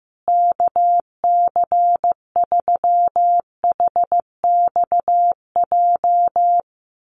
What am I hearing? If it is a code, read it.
Morse code, heard as a keyed tone, about 15 wpm, 700 Hz. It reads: KC3HXJ
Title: USA Call Signs Repeated 3x 15wpm